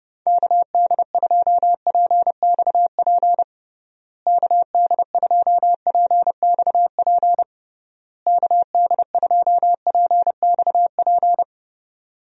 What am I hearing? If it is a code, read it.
Morse code, heard as a keyed tone, about 32 wpm, 700 Hz. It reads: KD2PXP KD2PXP KD2PXP
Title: USA Call Signs - 6 Characters - Repeated 3x - 30wpm